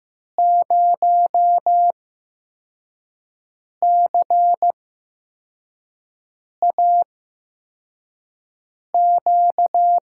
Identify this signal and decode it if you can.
Morse code, heard as a keyed tone, about 15 wpm, spaced as for 4 wpm, 700 Hz. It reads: 0CAQ
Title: USA Call Signs - 6 Characters - 8x Character Spacing - 15wpm